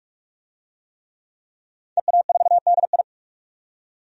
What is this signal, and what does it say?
Morse code, heard as a keyed tone, about 42 wpm, 700 Hz. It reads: EA4DI
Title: DXCC-100 Call Signs - 5 Characters - 45wpm